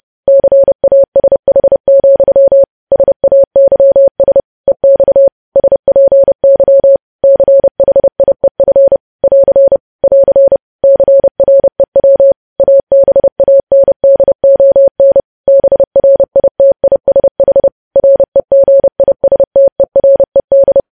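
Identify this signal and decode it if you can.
Morse code, heard as a keyed tone, about 30 wpm, 550 Hz. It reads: CASH, SAYS EX SPY CHIEF <AR> <AR> CREW ABANDON BRITISH REGISTERED
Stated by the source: Morse code news headlines practice sent at 30 wpm.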